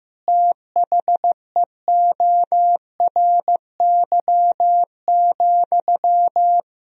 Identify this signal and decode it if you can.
Morse code, heard as a keyed tone, about 15 wpm, 700 Hz. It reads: THEORY,
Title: Sentences from Top 1000 Words - Review Part 3 - 15wpm